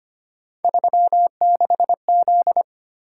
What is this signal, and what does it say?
Morse code, heard as a keyed tone, about 29 wpm, 700 Hz. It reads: 36Z
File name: Speed-Racing - Three Letter-Number Combinations - 25wpm